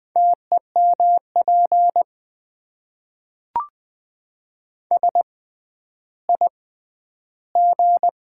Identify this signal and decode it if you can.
Morse code, heard as a keyed tone, about 20 wpm, spaced as for 11 wpm, 700 Hz. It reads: TEMP SIG
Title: Top 90 QSO Elements - 6x Character Spacing - 20wpm